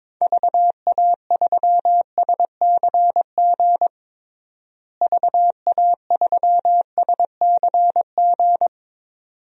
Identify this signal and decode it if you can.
Morse code, heard as a keyed tone, about 21 wpm, 700 Hz. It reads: VA3SCG VA3SCG
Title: DXCC-100 Call Signs - 6 Characters - Repeated 2x - 22wpm